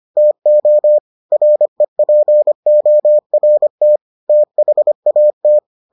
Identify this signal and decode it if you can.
Morse code, heard as a keyed tone, about 25 wpm, 600 Hz. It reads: TO REPORT THAT